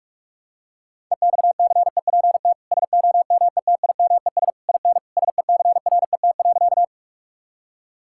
Morse code, heard as a keyed tone, about 46 wpm, 700 Hz. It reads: EXCEPT SOMETIMES IN SECRET.
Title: Sentences from Top 1600 Words 45wpm